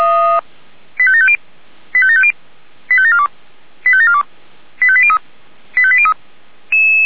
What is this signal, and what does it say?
Five-tone selective call (ZVEI), heard as quick runs of five tones, tones of about 70 ms. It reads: 85750 85750 85742 85742 85803 85803